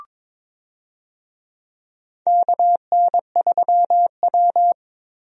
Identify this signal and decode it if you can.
Morse code, heard as a keyed tone, about 22 wpm, 700 Hz. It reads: KN3W
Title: USA Call Signs - 4 Characters - Repeated 2x - 22wpm